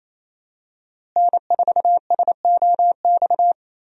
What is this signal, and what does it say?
Morse code, heard as a keyed tone, about 28 wpm, 700 Hz. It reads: N4SOX
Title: USA Call Signs - 5 Characters - Repeated 3x - 28wpm